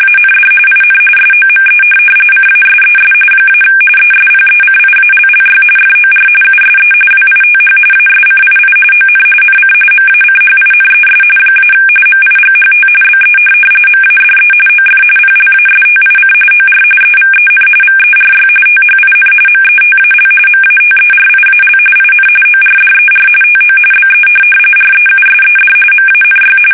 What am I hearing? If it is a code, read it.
RTTY, about 45.45 baud. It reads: AND IN SOME CASES - ESPECIALLY FOR USE WITH ASCII ON NC MACHINES - MIGHT BE MADE OF PLASTIC OR EVEN VERY THIN METAL MATERIAL IN ORDER TO BE REUSED MANY
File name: fsk850 / teletype-RTTY